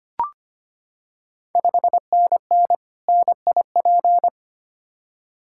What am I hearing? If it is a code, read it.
Morse code, heard as a keyed tone, about 25 wpm, 700 Hz. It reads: 5NN NIP
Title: Ontario QSO Party Exchange 25wpm